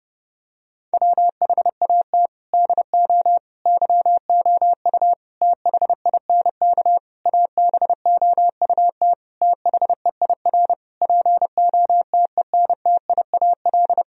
Morse code, heard as a keyed tone, about 29 wpm, 700 Hz. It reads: WHAT DO YOU THINK ABOUT THEIR POTENTIAL
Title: Sentences from Top 1000 Words - Review Part 4 - 30wpm